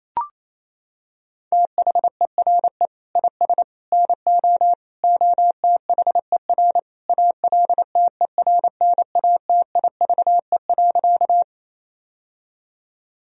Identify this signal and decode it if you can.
Morse code, heard as a keyed tone, about 28 wpm, 700 Hz. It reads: THERE IS NO OTHER ALTERNATIVE.
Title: Sentences from Top 1000 Words - Review Part 4 - 28wpm